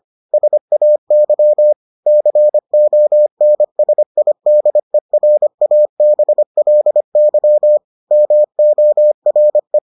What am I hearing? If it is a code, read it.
Morse code, heard as a keyed tone, about 25 wpm, 600 Hz. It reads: SAY CONSIDERABLY MORE